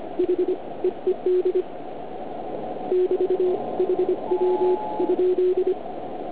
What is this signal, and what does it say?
Morse code, heard as a keyed tone, about 25 wpm, 360 Hz, with radio noise. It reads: H EED <BT> HW?